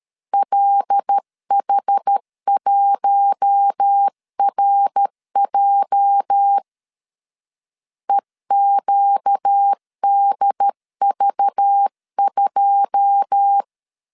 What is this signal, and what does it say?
Morse code, heard as a keyed tone, about 13 wpm, 800 Hz. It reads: LH1RJ EQDV2
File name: LEZIONI / 9 28 12wpm 2351